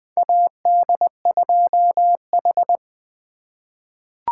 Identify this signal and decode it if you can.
Morse code, heard as a keyed tone, about 20 wpm, 700 Hz. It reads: AD2H
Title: USA Call Signs - 4 Characters - 2x Character Spacing - 20wpm